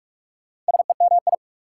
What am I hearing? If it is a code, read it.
Morse code, heard as a keyed tone, about 60 wpm, spaced as for 52 wpm, 690 Hz. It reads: SEMI
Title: Speed-Racing Prefixes and Suffixes 45wpm